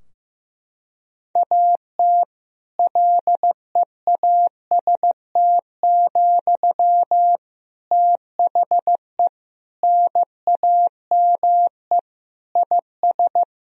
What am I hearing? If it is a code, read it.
Morse code, heard as a keyed tone, about 15 wpm, 700 Hz. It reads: AT LEAST, THE NAME IS
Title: Sentences from Top 1000 Words - Review Part 4 - 15wpm